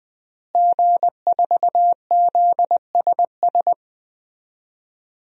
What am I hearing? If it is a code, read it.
Morse code, heard as a keyed tone, about 20 wpm, 700 Hz. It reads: G4ZSS
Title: DXCC-100 Call Signs - 5 Characters - Repeated 2x - 20wpm